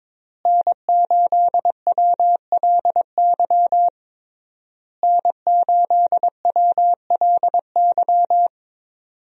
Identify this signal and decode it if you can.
Morse code, heard as a keyed tone, about 22 wpm, 700 Hz. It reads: N8WLY N8WLY
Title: USA Call Signs - 5 Characters - Repeated 3x - 22wpm